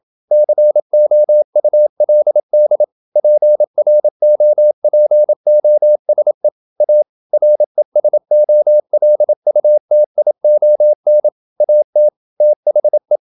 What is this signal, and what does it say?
Morse code, heard as a keyed tone, about 27 wpm, 600 Hz. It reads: COULD PROPOSE A RESOLUTION AT THE
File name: Never 0027wpm